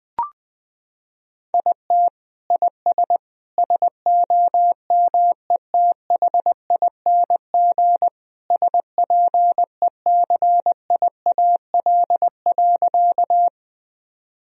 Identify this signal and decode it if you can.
Morse code, heard as a keyed tone, about 20 wpm, 700 Hz. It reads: IT IS SOMETHING SPECIAL.
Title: Sentences from Top 1000 Words - Review Part 3 - 20wpm